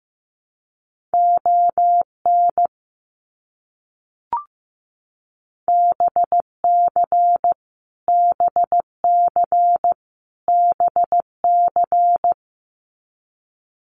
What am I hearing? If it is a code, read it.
Morse code, heard as a keyed tone, about 15 wpm, 700 Hz. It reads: ON BC BC BC
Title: Canadian Provinces and Territories - Repeated 3x - 15wpm